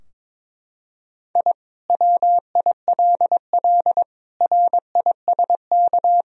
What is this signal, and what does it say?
Morse code, heard as a keyed tone, about 22 wpm, 700 Hz. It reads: I WILL RISK
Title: Sentences from Top 1000 Words - Review Part 4 - 22wpm